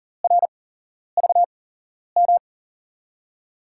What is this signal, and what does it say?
Morse code, heard as a keyed tone, about 40 wpm, 700 Hz. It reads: R V M
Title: USA Call Signs - 5 Characters - 8x Character Spacing - 40wpm